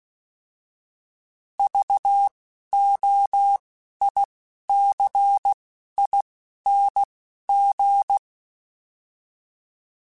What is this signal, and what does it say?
Morse code, heard as a keyed tone, about 16 wpm, spaced as for 11 wpm, 800 Hz. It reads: VOICING
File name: Alphabetical / Alphabetical mp3s / Vs @10